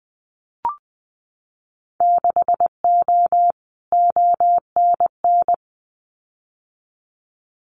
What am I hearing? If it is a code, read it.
Morse code, heard as a keyed tone, about 20 wpm, 700 Hz. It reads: 6O ONN
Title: Winter Field Day Exchanges Encore 20wpm